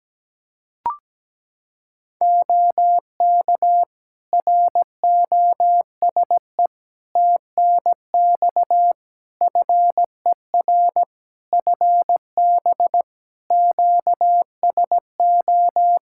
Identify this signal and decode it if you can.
Morse code, heard as a keyed tone, about 17 wpm, 700 Hz. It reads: OK ROSE TNX FER FB QSO
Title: Practice QSO - Exchange 4 - Ending - 17wpm